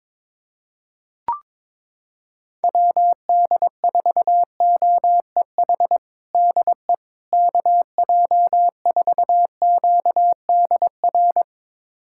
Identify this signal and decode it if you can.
Morse code, heard as a keyed tone, about 22 wpm, 700 Hz. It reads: WD4OEH DE KJ4QDR
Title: USA Callsign DE Callsign - 6 Characters - 22wpm